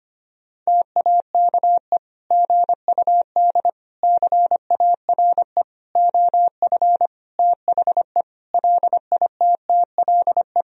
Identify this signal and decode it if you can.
Morse code, heard as a keyed tone, about 25 wpm, 700 Hz. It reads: TAKE GUD CARE OF THE LITTLE
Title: Sentences from Top 400 Words and Abbreviations 25wpm